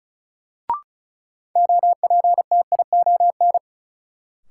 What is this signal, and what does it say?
Morse code, heard as a keyed tone, about 36 wpm, 700 Hz. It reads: OPTION
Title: Sentences from Top 1300 Words - New Word Drill - 35wpm